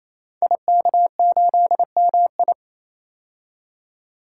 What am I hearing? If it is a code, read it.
Morse code, heard as a keyed tone, about 27 wpm, 700 Hz. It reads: IK8MI
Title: Code Groups - 5 Letter:Numbers - 28wpm